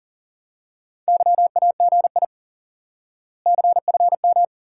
Morse code, heard as a keyed tone, about 38 wpm, 700 Hz. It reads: YAGI CFM
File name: IQE Course 3 - Lesson 89 - YAGI - Two Elements - 40wpm